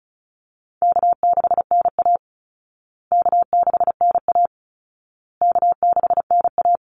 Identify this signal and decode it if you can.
Morse code, heard as a keyed tone, about 34 wpm, 700 Hz. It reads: K6NA K6NA K6NA
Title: Easy Call Signs Repeated 3x - 35wpm